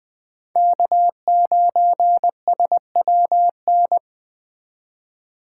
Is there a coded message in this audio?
Yes: Morse code from 0.6 to 4.0 s, heard as a keyed tone, about 20 wpm, 700 Hz. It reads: K9SWN